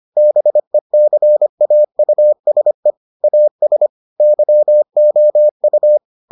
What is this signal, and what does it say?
Morse code, heard as a keyed tone, about 25 wpm, 600 Hz. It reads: BECAUSE AS YOU